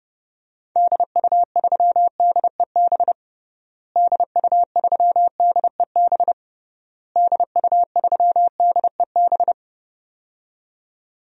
Morse code, heard as a keyed tone, about 32 wpm, 700 Hz. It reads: DU3DEB DU3DEB DU3DEB
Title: DXCC-100 Call Signs - 6 Characters - Repeated 3x - 30wpm